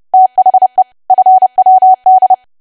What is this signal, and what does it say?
Morse code, heard as a keyed tone, about 30 wpm, 750 Hz. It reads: THE FWD
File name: cwa / W1AW-Level3 / 30WPM